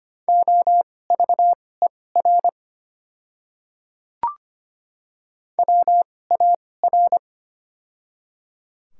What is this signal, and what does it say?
Morse code, heard as a keyed tone, about 25 wpm, spaced as for 18 wpm, 700 Hz. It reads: OVER WAR
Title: 2x Character Spacing - Top 500 Words - 25wpm